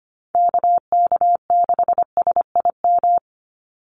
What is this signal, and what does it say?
Morse code, heard as a keyed tone, about 25 wpm, 700 Hz. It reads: KK6SIM
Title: USA Call Signs 25wpm